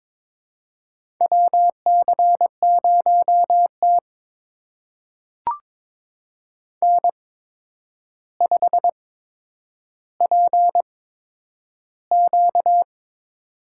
Morse code, heard as a keyed tone, about 22 wpm, spaced as for 9 wpm, 700 Hz. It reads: WC0T N5PQ
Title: USA Call Signs - 4 Characters - 8x Character Spacing - 22wpm